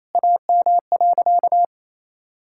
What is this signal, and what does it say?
Morse code, heard as a keyed tone, about 28 wpm, 700 Hz. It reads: AM.
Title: Sentences from Top 1000 Words - Review Part 6 - 28wpm